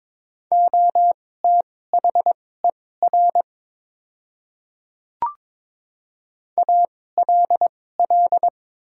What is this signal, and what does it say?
Morse code, heard as a keyed tone, about 23 wpm, spaced as for 16 wpm, 700 Hz. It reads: OTHER ALL
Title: 2x Character Spacing - Top 100 Words - 22wpm